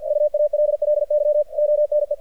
Morse code, heard as a keyed tone, about 50 wpm, 600 Hz, with radio noise. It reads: UALLY ONE
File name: archived / CRSnet